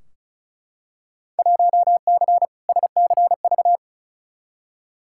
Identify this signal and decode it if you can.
Morse code, heard as a keyed tone, about 36 wpm, 700 Hz. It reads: 1C SCV
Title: Field Day Exchanges - 35wpm